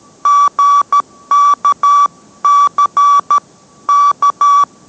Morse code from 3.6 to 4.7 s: K